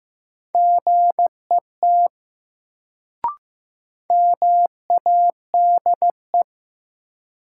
Morse code, heard as a keyed tone, about 15 wpm, 700 Hz. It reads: GET MADE
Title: Sentences from Top 200 Words - New Word Drill - 15wpm